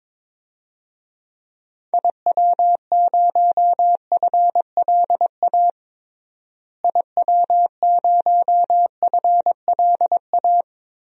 Morse code, heard as a keyed tone, about 22 wpm, 700 Hz. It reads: IW0FLA IW0FLA
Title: DXCC-100 Call Signs - 6 Characters - Repeated 3x - 22wpm